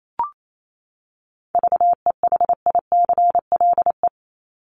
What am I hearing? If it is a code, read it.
Morse code, heard as a keyed tone, about 28 wpm, 700 Hz. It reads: VEHICLE
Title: Top 1751-2000 Words 28wpm